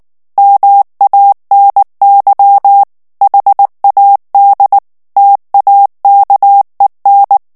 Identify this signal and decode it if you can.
Morse code, heard as a keyed tone, about 19 wpm, 800 Hz. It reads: MANY HAD TAKEN
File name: morse mp3 / ANGLAIS texte clair 2 vitesse 18 mots minute